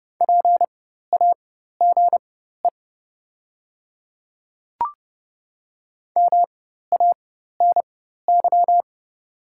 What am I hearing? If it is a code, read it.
Morse code, heard as a keyed tone, about 30 wpm, spaced as for 14 wpm, 700 Hz. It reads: PAGE MANY
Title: 4x Character Spacing - Top 300 Words - 30wpm